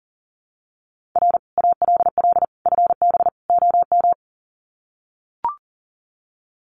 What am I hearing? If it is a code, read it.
Morse code, heard as a keyed tone, about 40 wpm, 700 Hz. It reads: R ALL FB OM
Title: Common QSO Phrases 40wpm